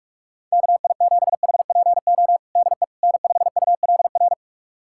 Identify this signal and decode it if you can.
Morse code, heard as a keyed tone, about 46 wpm, 700 Hz. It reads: KI7HPO DE N5ULR
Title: USA Callsign DE Callsign 45wpm